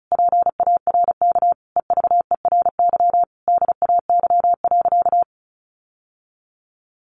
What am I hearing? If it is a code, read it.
Morse code, heard as a keyed tone, about 34 wpm, 700 Hz. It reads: PARK EVERY DAY.
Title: Sentences from Top 1300 Words 35wpm